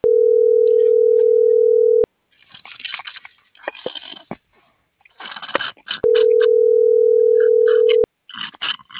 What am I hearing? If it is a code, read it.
Ringback tone (North America), 440 + 480 Hz.